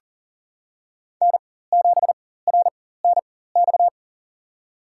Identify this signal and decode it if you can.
Morse code, heard as a keyed tone, about 40 wpm, spaced as for 19 wpm, 700 Hz. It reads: N7RNX
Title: USA Call Signs - 5 Characters - 4x Character Spacing - 40wpm